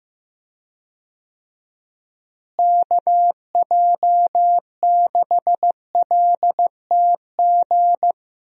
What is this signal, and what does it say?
Morse code, heard as a keyed tone, about 15 wpm, 700 Hz. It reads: KJ6LTG
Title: USA Call Signs - 6 Characters - Repeated 2x - 15wpm